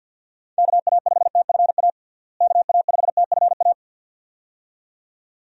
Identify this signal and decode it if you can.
Morse code, heard as a keyed tone, about 48 wpm, 700 Hz. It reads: KA5TFA KA5TFA
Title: USA Call Signs - 6 Characters - Repeated 2x - 50wpm